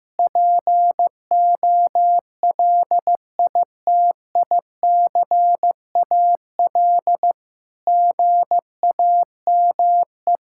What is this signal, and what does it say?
Morse code, heard as a keyed tone, about 16 wpm, 700 Hz. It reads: POLITICAL GAME
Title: Sentences from Top 1000 Words - Review Part 5 - 15wpm